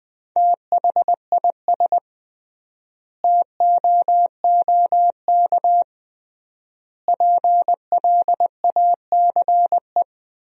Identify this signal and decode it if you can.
Morse code, heard as a keyed tone, about 20 wpm, 700 Hz. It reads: THIS TOOK PLACE
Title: 3x Word-Spacing - Sentences from Top 300 Words - 20wpm